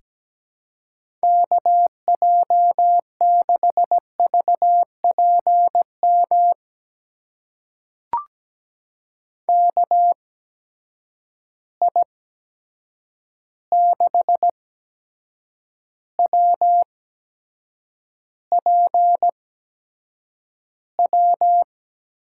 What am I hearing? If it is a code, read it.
Morse code, heard as a keyed tone, about 17 wpm, spaced as for 7 wpm, 700 Hz. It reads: KJ6VPM KI6WPW
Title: USA Call Signs - 6 Characters - 8x Character Spacing - 17wpm